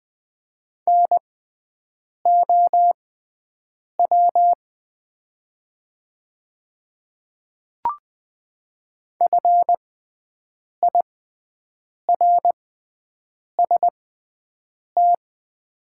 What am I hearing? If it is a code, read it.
Morse code, heard as a keyed tone, about 20 wpm, spaced as for 7 wpm, 700 Hz. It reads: NOW FIRST
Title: 6x Character Spacing - Top 200 Words - 20wpm